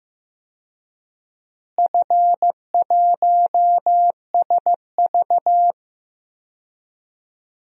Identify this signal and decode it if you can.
Morse code, heard as a keyed tone, about 15 wpm, 700 Hz. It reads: F1SV
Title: DXCC-100 Call Signs - 4 Characters - Repeated 3x - 15wpm